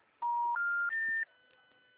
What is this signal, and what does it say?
Special information tone.
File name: out- -2007- - - .24875